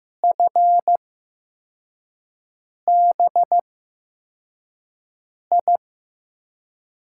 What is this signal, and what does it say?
Morse code, heard as a keyed tone, about 15 wpm, spaced as for 4 wpm, 700 Hz. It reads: FBI
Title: USA Call Signs - 6 Characters - 8x Character Spacing - 15wpm